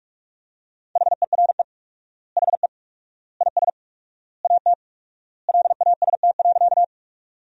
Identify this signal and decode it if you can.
Morse code, heard as a keyed tone, about 42 wpm, 700 Hz. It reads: HERE HE IS AT LAST.
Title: 4x Word-Spacing - Sentences from Top 300 Words - 45wpm